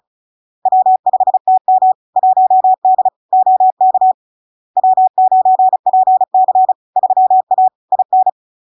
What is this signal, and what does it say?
Morse code, heard as a keyed tone, about 36 wpm, 750 Hz. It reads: W5TM 1D OK W9PC 3A IN
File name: LARK-FD-2022-Log-in-CW-35wpm-750-Hz